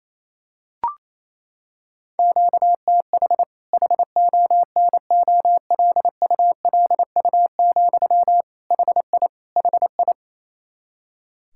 Morse code, heard as a keyed tone, about 28 wpm, 700 Hz. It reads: QTH HONOLULU, HI HI
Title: Practice QSO - Exchange 1 - Beginning - 28wpm